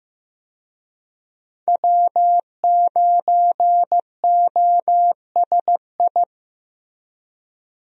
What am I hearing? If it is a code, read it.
Morse code, heard as a keyed tone, about 15 wpm, 700 Hz. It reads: W9OSI
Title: USA Call Signs - 5 Characters - Repeated 3x - 15wpm